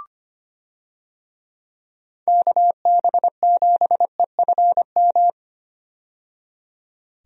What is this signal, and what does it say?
Morse code, heard as a keyed tone, about 25 wpm, 700 Hz. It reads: KB7EFM
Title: USA Call Signs - 6 Characters - 25wpm